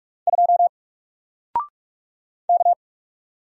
Morse code, heard as a keyed tone, about 44 wpm, 700 Hz. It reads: 2 K
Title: Single Letter-Number 45wpm